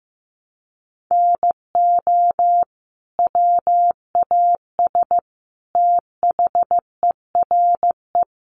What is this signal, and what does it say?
Morse code, heard as a keyed tone, about 14 wpm, 700 Hz. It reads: NO WAS THERE
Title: Sentences from Top 100 Words 15wpm